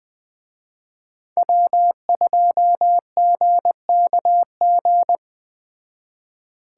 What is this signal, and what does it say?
Morse code, heard as a keyed tone, about 20 wpm, 700 Hz. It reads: W2GKG